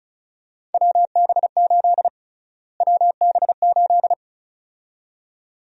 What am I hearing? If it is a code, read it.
Morse code, heard as a keyed tone, about 36 wpm, 700 Hz. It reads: WB8 WB8
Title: USA Call Sign Prefixes - Repeated 2x - 35wpm